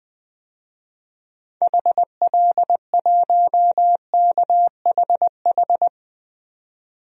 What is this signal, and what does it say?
Morse code, heard as a keyed tone, about 20 wpm, 700 Hz. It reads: HL1KHH
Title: DXCC-100 Call Signs - 6 Characters - 20wpm